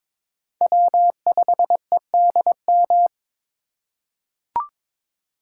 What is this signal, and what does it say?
Morse code, heard as a keyed tone, about 21 wpm, 690 Hz. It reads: W5EDM
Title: USA Call Signs - 5 Characters - 2x Character Spacing - 22wpm